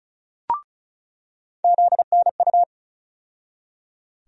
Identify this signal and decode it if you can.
Morse code, heard as a keyed tone, about 36 wpm, 700 Hz. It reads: ZNU
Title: USA Call Sign Postfixes - 3 Letters 35wpm